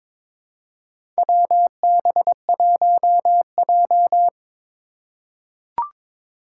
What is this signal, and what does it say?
Morse code, heard as a keyed tone, about 22 wpm, 700 Hz. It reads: WB1J
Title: USA Call Signs - 4 Characters - 4x Character Spacing - 22wpm